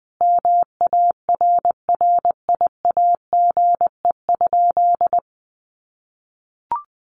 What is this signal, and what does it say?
Morse code, heard as a keyed tone, about 20 wpm, 700 Hz. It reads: MARRIAGE?
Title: Sentences from Top 1500 Words 20wpm